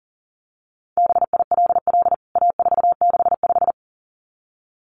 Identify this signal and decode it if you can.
Morse code, heard as a keyed tone, about 40 wpm, 700 Hz. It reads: BILL A465
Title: CWT Exchange 40wpm